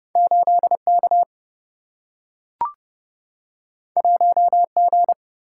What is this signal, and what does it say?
Morse code, heard as a keyed tone, about 30 wpm, 700 Hz. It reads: 8 K 1 G
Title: Single-Letter Single-Number 30wpm